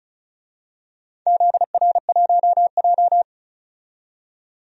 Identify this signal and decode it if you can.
Morse code, heard as a keyed tone, about 34 wpm, 700 Hz. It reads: ZR1J
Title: DXCC-100 Call Signs - 4 Characters - 35wpm